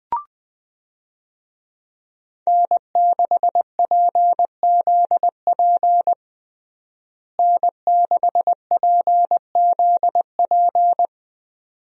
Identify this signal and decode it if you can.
Morse code, heard as a keyed tone, about 21 wpm, 700 Hz. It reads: N6PZP N6PZP
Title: USA Call Signs - 5 Characters - Repeated 2x - 20wpm